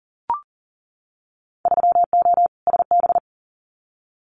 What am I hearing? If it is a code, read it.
Morse code, heard as a keyed tone, about 37 wpm, 700 Hz. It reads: 3O SB
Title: Winter Field Day Exchanges Encore 40wpm